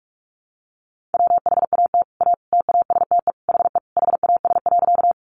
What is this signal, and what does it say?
Morse code, heard as a keyed tone, about 44 wpm, 700 Hz. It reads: WHAT A TASTE HE HAS.
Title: Sentences from Top 1100 Words 45wpm